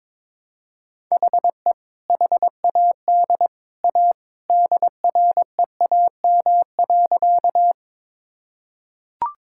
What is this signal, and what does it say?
Morse code, heard as a keyed tone, about 22 wpm, 700 Hz. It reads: HE HAD A DREAM.
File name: practice / 22wpm / Sentences with 2-Word Phrases - 22wpm